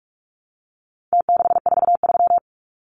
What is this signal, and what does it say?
Morse code, heard as a keyed tone, about 44 wpm, 700 Hz. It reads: T643
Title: Numbers - Contest Style - 4-Digit 45wpm